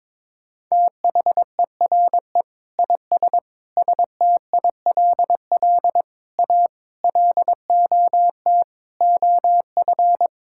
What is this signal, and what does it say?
Morse code, heard as a keyed tone, about 22 wpm, 700 Hz. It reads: THERE IS STILL A LOT OF
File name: Sets of 9 Words - 22wpm